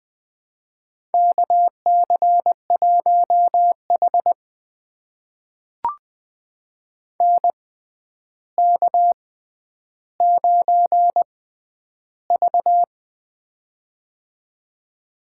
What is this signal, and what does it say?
Morse code, heard as a keyed tone, about 21 wpm, spaced as for 10 wpm, 700 Hz. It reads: KC1H NK9V
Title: USA Call Signs - 4 Characters - 6x Character Spacing - 20wpm